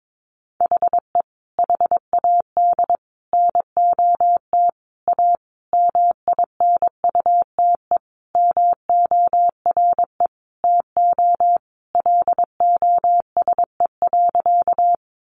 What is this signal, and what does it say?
Morse code, heard as a keyed tone, about 22 wpm, 700 Hz. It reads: HE HAD NOT A MINUTE MORE TO LOSE.
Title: Sentences from Top 1100 Words 22wpm